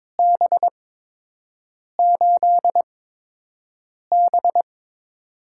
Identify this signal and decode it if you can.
Morse code, heard as a keyed tone, about 22 wpm, spaced as for 6 wpm, 700 Hz. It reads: B8B
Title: USA Call Signs - 8x Character Spacing - 22wpm